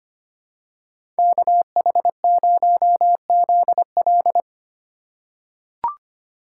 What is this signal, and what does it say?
Morse code, heard as a keyed tone, about 25 wpm, 700 Hz. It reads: KH0ZL
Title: USA Call Signs - 5 Characters - 4x Character Spacing - 25wpm